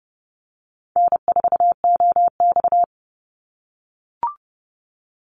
Morse code, heard as a keyed tone, about 32 wpm, 700 Hz. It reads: N4OX
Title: Easy Call Signs - 30wpm